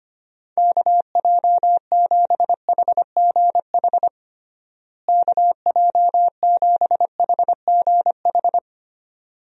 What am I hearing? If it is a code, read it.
Morse code, heard as a keyed tone, about 25 wpm, 700 Hz. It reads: KJ7HGH KJ7HGH
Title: USA Call Signs - 6 Characters - Repeated 2x - 25wpm